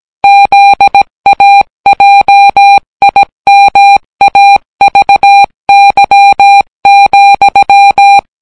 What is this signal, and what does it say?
Morse code, heard as a keyed tone, about 18 wpm, 800 Hz. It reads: ZAJIMAVY,